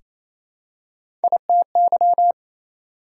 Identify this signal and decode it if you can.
Morse code, heard as a keyed tone, about 28 wpm, 700 Hz. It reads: ITY
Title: Prefixes and Suffixes 28wpm